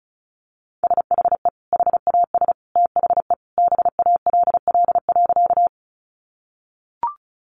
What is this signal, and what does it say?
Morse code, heard as a keyed tone, about 34 wpm, 700 Hz. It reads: SHE HAS THE BALL.
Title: Sentences from Top 900 Words 35wpm